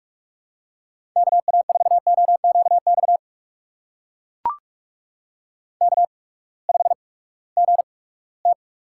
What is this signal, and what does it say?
Morse code, heard as a keyed tone, about 44 wpm, spaced as for 22 wpm, 700 Hz. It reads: KA4OQX K5GT
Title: USA Call Signs - 8x Character Spacing - 45wpm